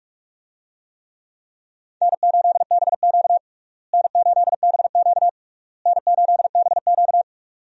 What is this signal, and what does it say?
Morse code, heard as a keyed tone, about 46 wpm, 700 Hz. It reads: N8BQ N8BQ N8BQ
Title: USA Call Signs - 4 Characters - Repeated 3x - 45wpm